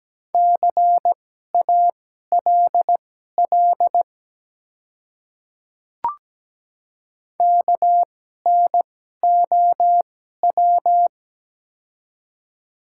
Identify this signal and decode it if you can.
Morse code, heard as a keyed tone, about 17 wpm, spaced as for 12 wpm, 700 Hz. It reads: CALL KNOW
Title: 2x Character Spacing - Top 100 Words - 17wpm